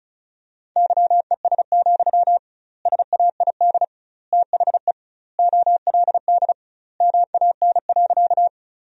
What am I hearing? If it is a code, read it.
Morse code, heard as a keyed tone, about 36 wpm, 700 Hz. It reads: YES, SAID THE OLD MAN.
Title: 2x Word-Spacing - Sentences from Top 500 Words - 35wpm